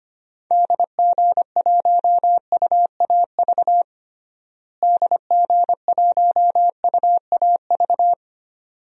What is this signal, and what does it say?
Morse code, heard as a keyed tone, about 25 wpm, 700 Hz. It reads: DG1UAV DG1UAV
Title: DXCC-100 Call Signs - 6 Characters - Repeated 2x - 25wpm